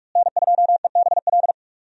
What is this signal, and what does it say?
Morse code, heard as a keyed tone, about 47 wpm, 700 Hz. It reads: N2EBL
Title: USA Call Signs - 5 Characters - Repeated 3x - 45wpm